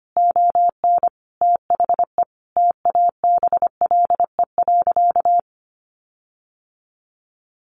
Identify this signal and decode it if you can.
Morse code, heard as a keyed tone, about 25 wpm, 700 Hz. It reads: ON THE TABLE.
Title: Sentences from Top 1400 Words 25wpm